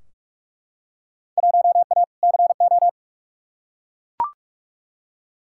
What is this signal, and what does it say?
Morse code, heard as a keyed tone, about 41 wpm, 700 Hz. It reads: 1A CO
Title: Field Day Exchanges - 45wpm